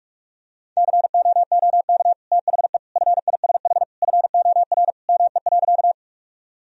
Morse code, heard as a keyed tone, about 46 wpm, 700 Hz. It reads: COOK THE FISH FOR ME.
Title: Sentences from Top 1000 Words - Review Part 4 - 45wpm